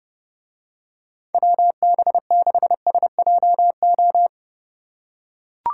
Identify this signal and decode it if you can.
Morse code, heard as a keyed tone, about 30 wpm, 700 Hz. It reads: WB6SJO